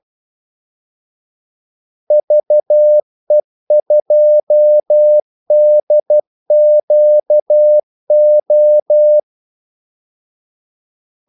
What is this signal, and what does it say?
Morse code, heard as a keyed tone, about 12 wpm, 600 Hz. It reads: VE2DQO
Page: 2 ekstra ordmellemrum mellem kaldesignalerne